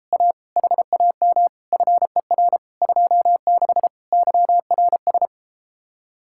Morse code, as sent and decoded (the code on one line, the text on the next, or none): .- / .... .- -- / ..-. .-. / ..--- -.... / -.-- .-. ...
A HAM FER 26 YRS